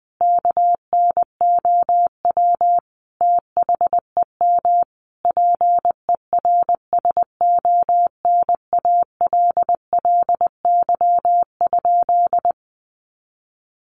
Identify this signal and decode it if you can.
Morse code, heard as a keyed tone, about 20 wpm, 700 Hz. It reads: KNOW THEM PERSONALLY?
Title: Sentences from Top 1400 Words 20wpm